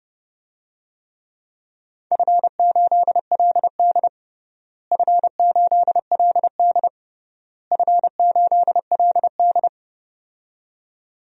Morse code, heard as a keyed tone, about 30 wpm, 700 Hz. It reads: F8LD F8LD F8LD
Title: DXCC-100 Call Signs - 4 Characters - Repeated 3x - 30wpm